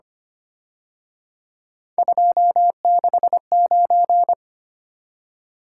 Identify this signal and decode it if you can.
Morse code, heard as a keyed tone, about 29 wpm, 700 Hz. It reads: 269
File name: Speed-Racing - Three Letter-Number Combinations - 25wpm